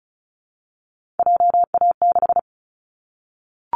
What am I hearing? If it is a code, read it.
Morse code, heard as a keyed tone, about 34 wpm, 700 Hz. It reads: JA6
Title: Callsign Prefixes for Top 25 DXCC - Repeated 3x - 35wpm